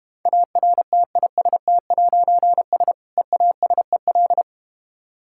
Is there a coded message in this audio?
Yes: Morse code from 0.3 to 4.4 s, heard as a keyed tone, about 32 wpm, 700 Hz. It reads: ARTIST'S EASEL